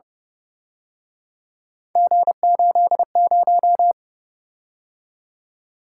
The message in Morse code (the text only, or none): G80